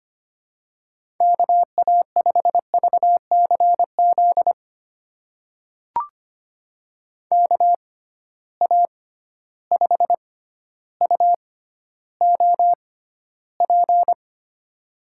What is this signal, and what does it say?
Morse code, heard as a keyed tone, about 25 wpm, spaced as for 13 wpm, 700 Hz. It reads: KA5VCZ KA5UOP
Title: USA Call Signs - 6 Characters - 6x Character Spacing - 25wpm